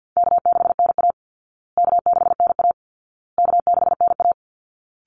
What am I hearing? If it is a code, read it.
Morse code, heard as a keyed tone, about 50 wpm, 700 Hz. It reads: K6NA K6NA K6NA
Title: Easy Call Signs Repeated 3x - 50wpm